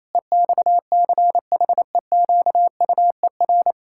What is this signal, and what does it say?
Morse code, heard as a keyed tone, about 28 wpm, 700 Hz. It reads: EXCHEQUER
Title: Pangram Challenge v2 - 28wpm